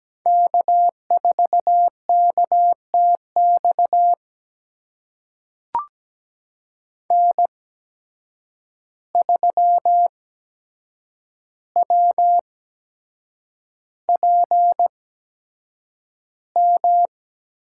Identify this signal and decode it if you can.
Morse code, heard as a keyed tone, about 17 wpm, spaced as for 7 wpm, 700 Hz. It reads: K4KTX N3WPM
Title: USA Call Signs - 5 Characters - 8x Character Spacing - 17wpm